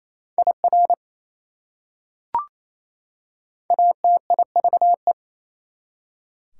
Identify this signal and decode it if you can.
Morse code, heard as a keyed tone, about 28 wpm, 700 Hz. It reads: IR ATIVE
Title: Prefixes and Suffixes 28wpm